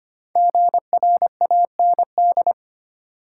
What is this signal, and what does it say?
Morse code, heard as a keyed tone, about 25 wpm, 700 Hz. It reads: GRAND
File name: 5 Letter Words - First 3 Letters Given - 25wpm